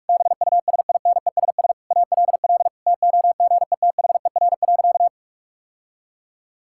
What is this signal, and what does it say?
Morse code, heard as a keyed tone, about 46 wpm, 700 Hz. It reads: BUSINESS ALL TOGETHER.
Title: Sentences from Top 1000 Words - Review Part 6 - 45wpm